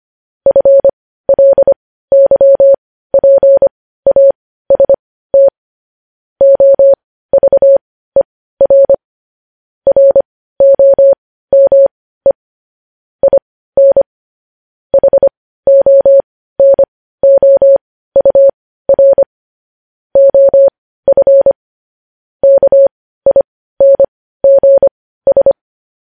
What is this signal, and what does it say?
Morse code, heard as a keyed tone, about 25 wpm, spaced as for 15 wpm, 550 Hz. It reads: FLYPAST OVER ROME IN HONOUR OF KINGS
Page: news-headlines-in-morse-code-at-15-wpm